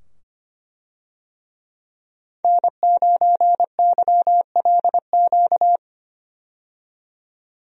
Morse code, heard as a keyed tone, about 25 wpm, 700 Hz. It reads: N9YLQ
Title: USA Call Signs - 5 Characters - 25wpm